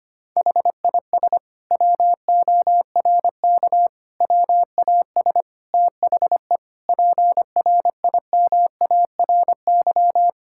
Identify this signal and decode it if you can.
Morse code, heard as a keyed tone, about 25 wpm, 700 Hz. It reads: HIS WORK WAS THE PRIMARY
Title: Sentences from Top 1000 Words - Review Part 5 - 25wpm